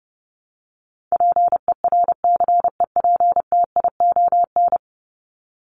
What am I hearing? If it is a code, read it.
Morse code, heard as a keyed tone, about 29 wpm, 700 Hz. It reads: PERCEPTION
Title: Top 1751-2000 Words 30wpm